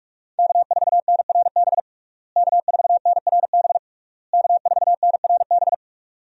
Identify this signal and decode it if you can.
Morse code, heard as a keyed tone, about 46 wpm, 700 Hz. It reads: K4NRB K4NRB K4NRB
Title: USA Call Signs - 5 Characters - Repeated 3x - 45wpm